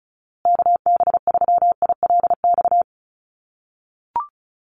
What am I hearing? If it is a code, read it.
Morse code, heard as a keyed tone, about 34 wpm, 700 Hz. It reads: KB3ILX
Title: USA Call Signs Repeated 3x 35wpm